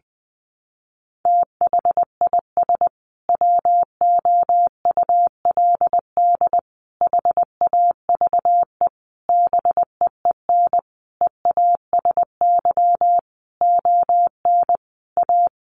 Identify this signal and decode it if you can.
Morse code, heard as a keyed tone, about 20 wpm, 700 Hz. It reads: THIS WOULD HAVE BEEN EASY ON A
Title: Sentences from Top 800 Words 20wpm